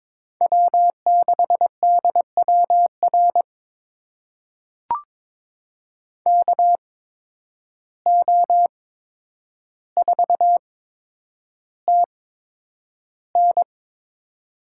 Morse code, heard as a keyed tone, about 22 wpm, spaced as for 9 wpm, 700 Hz. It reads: W6DWR KO4TN
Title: USA Call Signs - 5 Characters - 8x Character Spacing - 22wpm